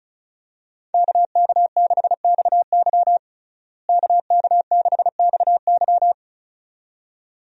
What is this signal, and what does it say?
Morse code, heard as a keyed tone, about 34 wpm, 700 Hz. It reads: KK6XY KK6XY
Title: USA Call Signs - 5 Characters - Repeated 2x - 35wpm